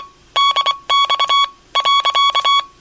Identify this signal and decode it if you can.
Morse code, heard as a keyed tone, about 21 wpm, 1120 Hz. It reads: DX .
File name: Aniversario 25 anos Encontro Dx- Radio aparecida